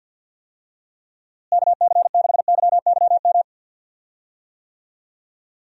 Morse code, heard as a keyed tone, about 51 wpm, 700 Hz. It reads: KC6YYM
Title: USA Call Signs - 6 Characters - Repeated 2x - 50wpm